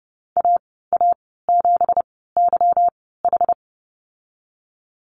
Morse code, heard as a keyed tone, about 30 wpm, spaced as for 17 wpm, 700 Hz. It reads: AA7YH
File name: Lesson 37g - Letter x - Call Signs 30wpm